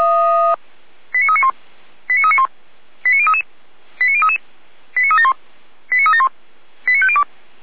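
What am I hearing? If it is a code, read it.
Five-tone selective call (ZVEI), heard as quick runs of five tones, tones of about 80 ms. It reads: 89391 89391 89033 89033 89371 89371 89502
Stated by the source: ZVEI five-tone call received on SDR.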